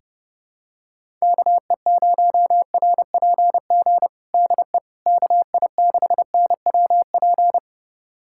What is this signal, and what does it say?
Morse code, heard as a keyed tone, about 32 wpm, 700 Hz. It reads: KE0RPG DE KI6NWP
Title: USA Callsign DE Callsign - 6 Characters - 30wpm